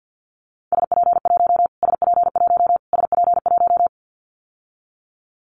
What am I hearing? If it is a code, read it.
Morse code, heard as a keyed tone, about 50 wpm, 700 Hz. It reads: SP1 SP1 SP1
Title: Callsign Prefixes for Top 25 DXCC - Repeated 3x - 50wpm